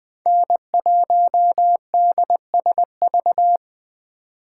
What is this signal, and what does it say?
Morse code, heard as a keyed tone, about 20 wpm, 700 Hz. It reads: N1DSV
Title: Speed Racing - USA 1x3 Call Signs 17wpm